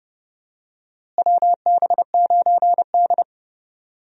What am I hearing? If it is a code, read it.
Morse code, heard as a keyed tone, about 30 wpm, 700 Hz. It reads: WB9D
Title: USA Call Signs - 4 Characters - Repeated 2x - 30wpm